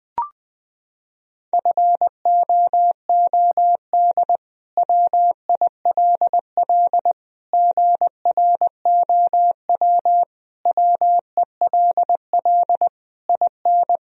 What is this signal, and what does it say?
Morse code, heard as a keyed tone, about 20 wpm, 700 Hz. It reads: FOOD WILL GROW WELL IN
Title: Sentences from Top 1000 Words - Review Part 2 - 20wpm